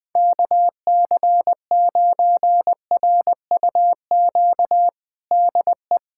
Morse code, heard as a keyed tone, about 20 wpm, 700 Hz. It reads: KC9RUQ DE
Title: USA Callsign DE Callsign - 6 Characters - 20wpm